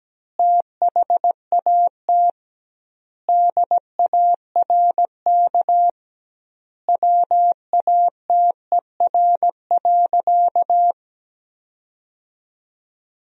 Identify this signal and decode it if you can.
Morse code, heard as a keyed tone, about 17 wpm, 700 Hz. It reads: THAT DARK WATER.
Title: 2x Word-Spacing - Sentences from Top 500 Words - 17wpm